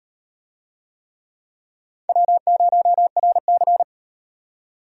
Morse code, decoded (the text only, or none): W0RC